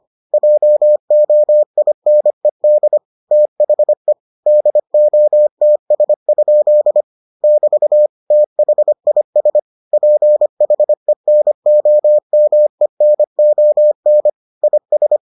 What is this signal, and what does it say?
Morse code, heard as a keyed tone, about 25 wpm, 600 Hz. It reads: JOINED THE DOTS? <BT> THIS PHENOMENON IS